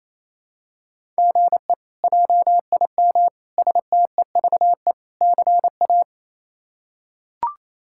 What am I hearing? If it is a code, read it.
Morse code, heard as a keyed tone, about 27 wpm, 700 Hz. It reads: GE JIM STEVE CA
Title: K1USN SST QSO Practice - 28wpm